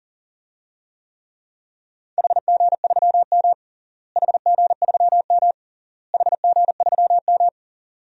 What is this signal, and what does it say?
Morse code, heard as a keyed tone, about 37 wpm, 700 Hz. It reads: HG3M HG3M HG3M
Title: DXCC-100 Call Signs - 4 Characters - Repeated 3x - 40wpm